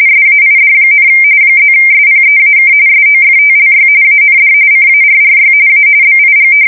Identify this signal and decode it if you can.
RTTY, about 45.45 baud. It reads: SAID, +SPEAK SOFTLY AND CARRY A BIG
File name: fsk170 / usnavy3